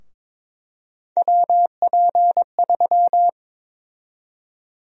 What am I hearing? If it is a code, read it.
Morse code, heard as a keyed tone, about 22 wpm, 700 Hz. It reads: WP3
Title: USA Call Sign Prefixes - 22wpm